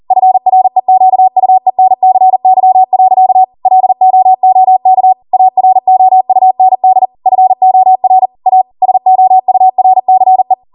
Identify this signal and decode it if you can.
Morse code, heard as a keyed tone, about 40 wpm, 750 Hz. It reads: FREQUENCY. LOOK AROUND FOR A SOURCE
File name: Morse / 40WPM